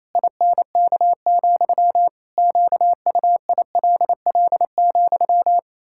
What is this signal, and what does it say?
Morse code, heard as a keyed tone, about 28 wpm, 700 Hz. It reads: INK, QUILL,
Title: Pangram Challenge v2 - 28wpm